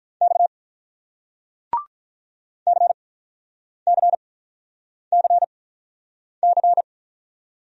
Morse code, heard as a keyed tone, about 47 wpm, spaced as for 9 wpm, 700 Hz. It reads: X CCCC